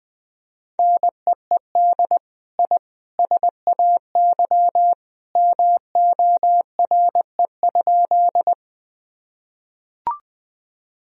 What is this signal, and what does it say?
Morse code, heard as a keyed tone, about 20 wpm, 700 Hz. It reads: NEED I SAY MORE?
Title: Sentences from Top 1000 Words - Review Part 1 - 20wpm